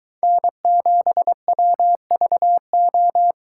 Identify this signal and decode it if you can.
Morse code, heard as a keyed tone, about 23 wpm, 700 Hz. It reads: N7WVO